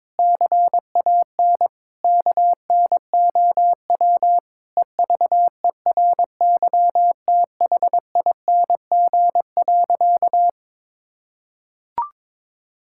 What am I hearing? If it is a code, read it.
Morse code, heard as a keyed tone, about 21 wpm, 700 Hz. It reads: CAN KNOW EVERYTHING.
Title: Sentences from Top 1600 Words 22wpm